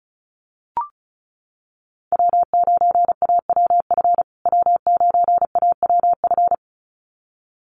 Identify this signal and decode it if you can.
Morse code, heard as a keyed tone, about 34 wpm, 700 Hz. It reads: W9AWF W9AWF
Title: USA Call Signs Repeated 2x 35wpm